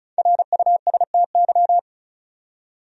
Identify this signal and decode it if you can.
Morse code, heard as a keyed tone, about 35 wpm, 700 Hz. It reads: RUSTY
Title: USA Common Names - Max 5 Letters - 35wpm